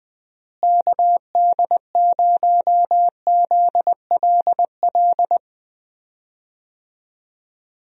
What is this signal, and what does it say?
Morse code, heard as a keyed tone, about 20 wpm, 700 Hz. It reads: KD0ZLL
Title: USA Call Signs - 6 Characters - Repeated 2x - 20wpm